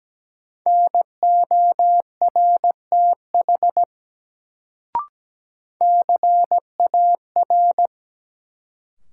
Morse code, heard as a keyed tone, about 17 wpm, 700 Hz. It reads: NORTH CAR
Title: Sets of 5 Words Encore - New Word Drill - 17wpm